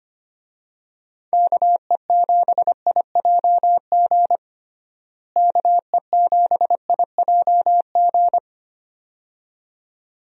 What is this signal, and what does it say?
Morse code, heard as a keyed tone, about 25 wpm, 700 Hz. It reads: KE7IJG KE7IJG
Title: USA Call Signs - 6 Characters - Repeated 2x - 25wpm